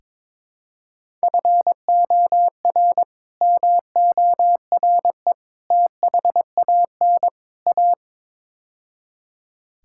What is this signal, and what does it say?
Morse code, heard as a keyed tone, about 23 wpm, 700 Hz. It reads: FOR MORE THAN A
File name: practice / 22wpm / Build A Phrase - 5-Word Phrases - 22wpm